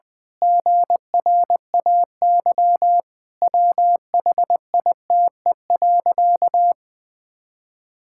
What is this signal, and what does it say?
Morse code, heard as a keyed tone, about 20 wpm, 700 Hz. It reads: GRAY WHITE.
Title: Pangram Challenge v2 - 20wpm